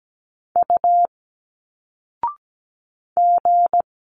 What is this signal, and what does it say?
Morse code, heard as a keyed tone, about 17 wpm, 700 Hz. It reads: U G
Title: Single Letters 17wpm